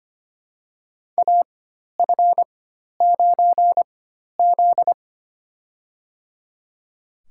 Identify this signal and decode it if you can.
Morse code, heard as a keyed tone, about 25 wpm, spaced as for 12 wpm, 700 Hz. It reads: AF9Z
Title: USA Call Signs - 4 Characters - 4x Character Spacing - 25wpm